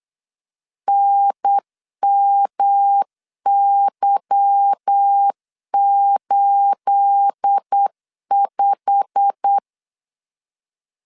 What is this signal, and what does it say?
Morse code, heard as a keyed tone, about 8 wpm, 800 Hz. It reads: NMY85
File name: LEZIONI / 9 28 8wpm 3512